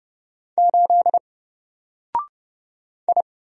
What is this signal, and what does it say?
Morse code, heard as a keyed tone, about 30 wpm, 700 Hz. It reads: 8 I
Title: Single Letter-Number Alt 30wpm